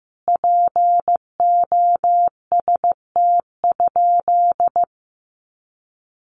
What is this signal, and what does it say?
Morse code, heard as a keyed tone, about 15 wpm, 700 Hz. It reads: POST?
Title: Sentences from Top 1000 Words 15wpm